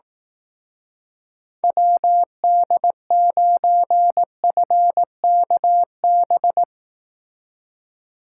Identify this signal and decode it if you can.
Morse code, heard as a keyed tone, about 18 wpm, 700 Hz. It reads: WD9FKB
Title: Speed Racing - USA 2x3 Call Signs 15wpm